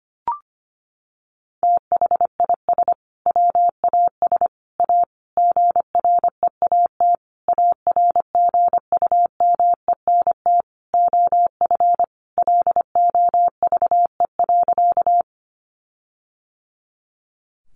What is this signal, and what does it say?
Morse code, heard as a keyed tone, about 25 wpm, 700 Hz. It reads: THIS WAS A GREAT ARGUMENT OF LOVE.
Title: Sentences from Top 1400 Words 25wpm